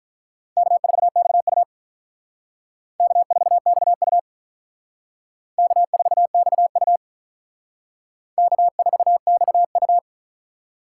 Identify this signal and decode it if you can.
Morse code, heard as a keyed tone, about 46 wpm, 700 Hz. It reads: K4XU K4XU K4XU K4XU